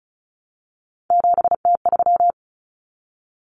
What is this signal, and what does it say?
Morse code, heard as a keyed tone, about 34 wpm, 700 Hz. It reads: 7T3
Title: Common Cut Numbers - 3-Digit 35wpm